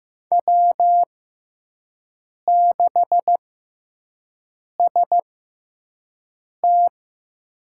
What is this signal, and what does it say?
Morse code, heard as a keyed tone, about 15 wpm, spaced as for 5 wpm, 700 Hz. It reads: W6ST
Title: USA Call Signs - 6x Character Spacing - 15wpm